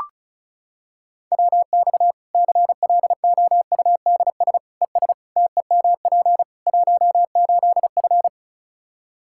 Morse code, heard as a keyed tone, about 34 wpm, 700 Hz. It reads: WX CLOUDS ES TEMP 18F
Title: Practice QSO - Exchange 2 - Info - 35wpm